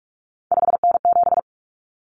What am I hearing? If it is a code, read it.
Morse code, heard as a keyed tone, about 44 wpm, 700 Hz. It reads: 5N7
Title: Common Cut Numbers - 3-Digit 45wpm